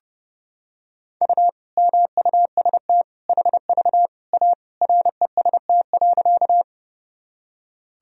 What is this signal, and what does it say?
Morse code, heard as a keyed tone, about 30 wpm, 700 Hz. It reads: U MUST HV A REST.
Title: Sentences from Top 500 Words and Abbreviations 30wpm